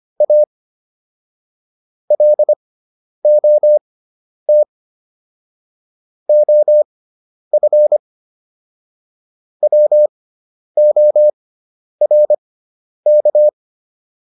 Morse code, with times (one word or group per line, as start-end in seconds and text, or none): 0.2-0.4 A
2.1-4.6 LOT
6.3-8.0 OF
9.6-13.5 WORK